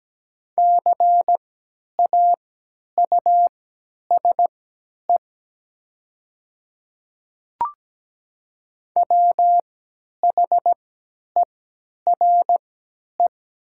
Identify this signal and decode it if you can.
Morse code, heard as a keyed tone, about 17 wpm, spaced as for 10 wpm, 700 Hz. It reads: CAUSE WHERE
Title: 3x Character Spacing - Top 300 Words - 17wpm